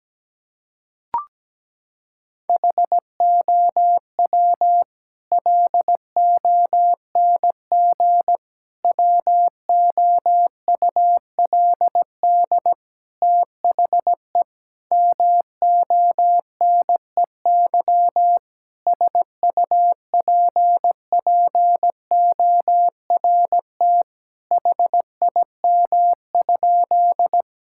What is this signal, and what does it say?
Morse code, heard as a keyed tone, about 17 wpm, 700 Hz. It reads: HOW LONG WOULD THE MONEY SUPPORT HIM?
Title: Sentences from Top 1000 Words - Review Part 5 - 17wpm